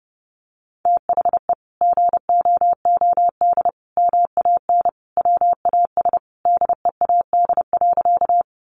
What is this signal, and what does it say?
Morse code, heard as a keyed tone, about 30 wpm, 700 Hz. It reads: THE GOOD MAN WAS DEAD.
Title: Sentences from Top 1500 Words 30wpm